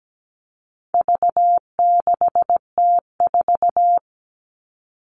Morse code, heard as a keyed tone, about 17 wpm, 700 Hz. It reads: V6T4
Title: All Cut Numbers - 4-Digit 17wpm